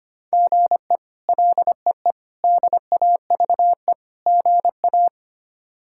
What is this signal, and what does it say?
Morse code, heard as a keyed tone, about 25 wpm, 700 Hz. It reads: GE LEE DAVE GA
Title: K1USN SST QSO Practice - 25wpm